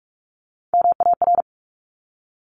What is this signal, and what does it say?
Morse code, heard as a keyed tone, about 46 wpm, 700 Hz. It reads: MAR
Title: ARRL Sections 45wpm